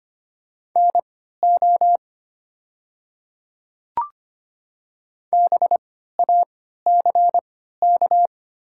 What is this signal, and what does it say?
Morse code, heard as a keyed tone, about 25 wpm, spaced as for 14 wpm, 700 Hz. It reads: NO BACK
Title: 3x Character Spacing - Top 300 Words - 25wpm